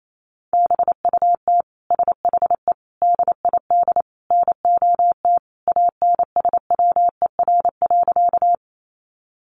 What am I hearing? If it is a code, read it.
Morse code, heard as a keyed tone, about 28 wpm, 700 Hz. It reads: BUT SHE DID NOT ANSWER.
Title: Sentences from Top 600 Words 28wpm